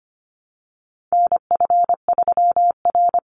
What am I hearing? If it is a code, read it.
Morse code, heard as a keyed tone, about 25 wpm, 700 Hz. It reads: NF3R
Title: Call Signs Repeated 2x - 25wpm